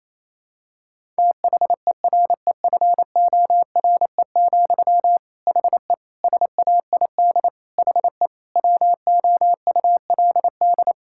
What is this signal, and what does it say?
Morse code, heard as a keyed tone, about 28 wpm, 700 Hz. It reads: THEREFORE, HE SAID HE WOULD
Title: Sentences from Top 1000 Words - Review Part 5 - 28wpm